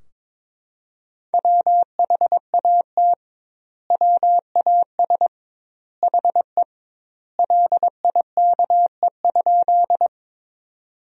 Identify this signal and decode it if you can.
Morse code, heard as a keyed tone, about 22 wpm, 700 Hz. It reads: WHAT WAS HE LIKE?
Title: 2x Word-Spacing - Sentences from Top 100 Words - 22wpm